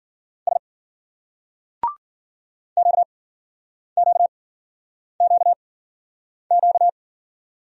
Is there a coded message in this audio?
Yes: Morse code from 0.5 to 6.9 s, heard as a keyed tone, about 53 wpm, spaced as for 9 wpm, 700 Hz. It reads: S QQQQ